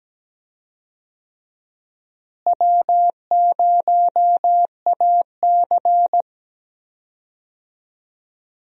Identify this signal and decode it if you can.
Morse code, heard as a keyed tone, about 17 wpm, 700 Hz. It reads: W0AC
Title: USA Call Signs - 4 Characters - Repeated 2x - 17wpm